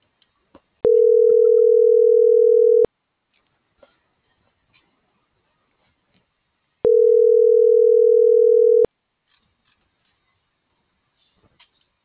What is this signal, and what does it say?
Ringback tone (North America), 440 + 480 Hz.